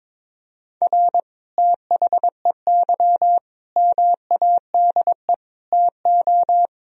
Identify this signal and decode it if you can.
Morse code, heard as a keyed tone, about 21 wpm, 700 Hz. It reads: R THEY MADE TO
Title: Sentences from Top 400 Words and Abbreviations 22wpm